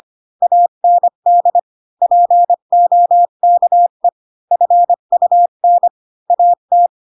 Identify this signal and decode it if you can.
Morse code, heard as a keyed tone, about 25 wpm, 700 Hz. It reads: AND POKE FUN AT
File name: Pinocchio / Pinocchio ch3 20 25 ik1hge